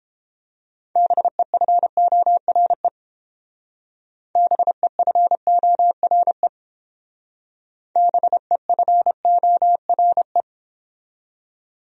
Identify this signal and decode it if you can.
Morse code, heard as a keyed tone, about 30 wpm, 700 Hz. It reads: BEFORE BEFORE BEFORE